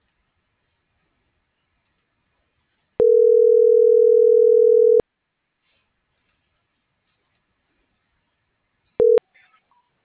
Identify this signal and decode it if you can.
Ringback tone (North America), 440 + 480 Hz.